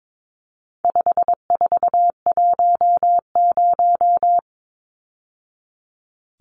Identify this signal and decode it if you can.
Morse code, heard as a keyed tone, about 22 wpm, 700 Hz. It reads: 5410
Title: Numbers - 4-Digit 22wpm